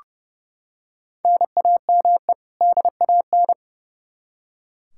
Morse code, heard as a keyed tone, about 32 wpm, 700 Hz. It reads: NAME DAN
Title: Practice QSO - Exchange 1 - Beginning - 30wpm